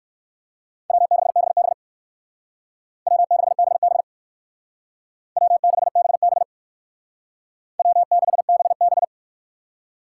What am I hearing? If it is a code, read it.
Morse code, heard as a keyed tone, about 57 wpm, 700 Hz. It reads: W6BB W6BB W6BB W6BB